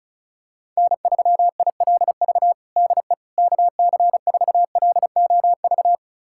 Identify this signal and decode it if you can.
Morse code, heard as a keyed tone, about 36 wpm, 700 Hz. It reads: N3ILV DE KC4LOV
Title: USA Callsign DE Callsign 35wpm